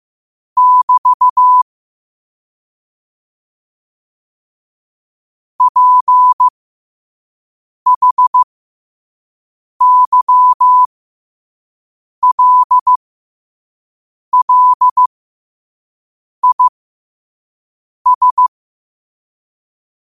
Morse code, heard as a keyed tone, about 15 wpm, spaced as for 5 wpm, 1000 Hz. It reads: <BT> PHYLLIS
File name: qotdfiles / 5wpm